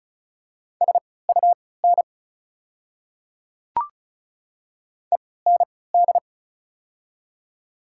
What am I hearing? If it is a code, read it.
Morse code, heard as a keyed tone, about 36 wpm, spaced as for 20 wpm, 700 Hz. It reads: SUN END
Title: 3x Character Spacing - Top 300 Words - 35wpm